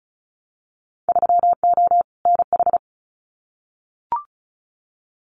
Morse code, heard as a keyed tone, about 34 wpm, 700 Hz. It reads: 3O NH
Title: Winter Field Day Exchanges Encore 35wpm